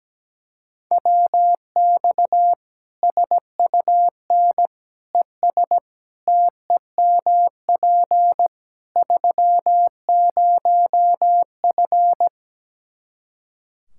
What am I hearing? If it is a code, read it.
Morse code, heard as a keyed tone, about 17 wpm, 700 Hz. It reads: WX SUN ES TEMP 30F
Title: Practice QSO - Exchange 2 - Info - 17wpm